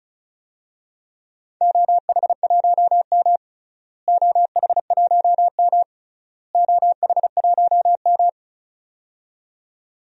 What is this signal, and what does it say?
Morse code, heard as a keyed tone, about 36 wpm, 700 Hz. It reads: OH1M OH1M OH1M
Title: DXCC-100 Call Signs - 4 Characters - Repeated 3x - 35wpm